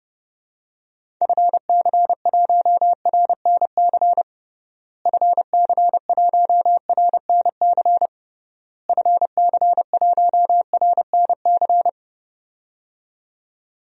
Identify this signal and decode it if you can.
Morse code, heard as a keyed tone, about 30 wpm, 700 Hz. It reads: FC1RNC FC1RNC FC1RNC
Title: DXCC-100 Call Signs - 6 Characters - Repeated 3x - 30wpm